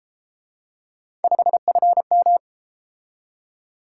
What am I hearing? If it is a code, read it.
Morse code, heard as a keyed tone, about 31 wpm, 700 Hz. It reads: 5FM